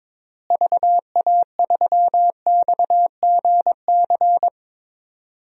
Morse code, heard as a keyed tone, about 22 wpm, 700 Hz. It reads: VA3XGC
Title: DXCC-100 Call Signs - 6 Characters - Repeated 3x - 22wpm